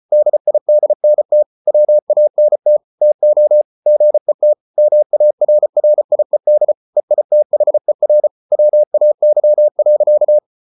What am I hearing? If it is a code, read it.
Morse code, heard as a keyed tone, about 34 wpm, 600 Hz. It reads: DIDNT WANT TO GET MARRIED EITHER WAY.